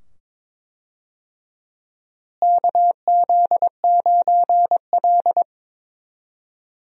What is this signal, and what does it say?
Morse code, heard as a keyed tone, about 22 wpm, 700 Hz. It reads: KZ9L
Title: USA Call Signs - 4 Characters - Repeated 2x - 22wpm